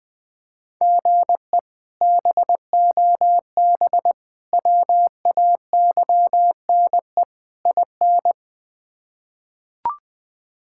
Morse code, heard as a keyed tone, about 20 wpm, 700 Hz. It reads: GE BOB WAYNE IN
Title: K1USN SST QSO Practice - 20wpm